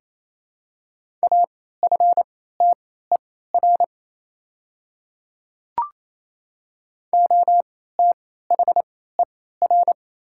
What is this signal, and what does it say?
Morse code, heard as a keyed tone, about 27 wpm, spaced as for 16 wpm, 690 Hz. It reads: AFTER OTHER
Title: 3x Character Spacing - Top 200 Words - 28wpm